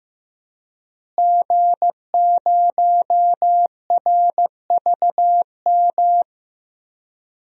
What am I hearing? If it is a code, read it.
Morse code, heard as a keyed tone, about 15 wpm, 700 Hz. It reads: G0RVM
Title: DXCC-100 Call Signs - 5 Characters - Repeated 3x - 15wpm